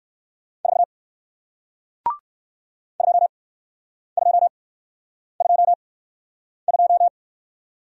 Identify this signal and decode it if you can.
Morse code, heard as a keyed tone, about 59 wpm, spaced as for 9 wpm, 700 Hz. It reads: 4 2222